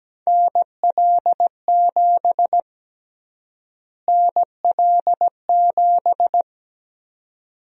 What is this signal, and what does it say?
Morse code, heard as a keyed tone, about 17 wpm, 700 Hz. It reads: NL7 NL7
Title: USA Call Sign Prefixes - Repeated 2x - 17wpm